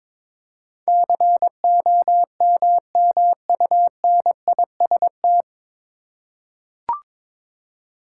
Morse code, heard as a keyed tone, about 21 wpm, 700 Hz. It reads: COMMUNIST
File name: 9 Letter Words - First 3 Letters Given - 22wpm